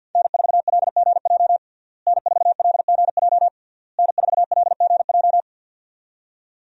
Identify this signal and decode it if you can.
Morse code, heard as a keyed tone, about 50 wpm, 700 Hz. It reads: N4LGJ N4LGJ N4LGJ
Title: USA Call Signs - 5 Characters - Repeated 3x - 50wpm